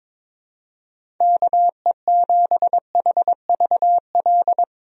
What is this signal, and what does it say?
Morse code, heard as a keyed tone, about 22 wpm, 700 Hz. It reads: KE7HVL
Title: USA Call Signs - 6 Characters - Repeated 2x - 22wpm